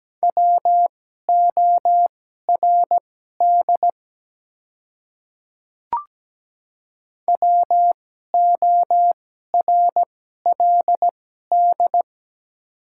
Morse code, heard as a keyed tone, about 17 wpm, spaced as for 12 wpm, 700 Hz. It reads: WORD WORLD
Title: 2x Character Spacing - Top 200 Words - 17wpm